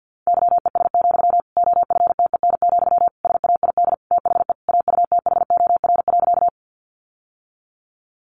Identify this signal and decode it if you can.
Morse code, heard as a keyed tone, about 50 wpm, 700 Hz. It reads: YES, OFTEN, SAID THE AUTHOR.
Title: Sentences from Top 1200 Words 50wpm